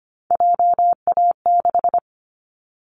Morse code, heard as a keyed tone, about 25 wpm, 700 Hz. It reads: JA6
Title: Callsign Prefixes for Top 25 DXCC - Repeated 3x - 25wpm